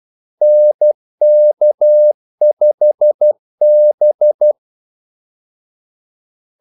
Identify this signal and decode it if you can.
Morse code, heard as a keyed tone, about 12 wpm, 600 Hz. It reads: NK5B
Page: 2 ekstra ordmellemrum mellem kaldesignalerne